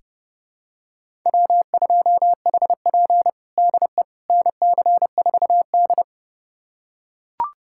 Morse code, heard as a keyed tone, about 32 wpm, 700 Hz. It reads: W2HP DE NC4D
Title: USA Callsign DE Callsign - 4 Characters - 30wpm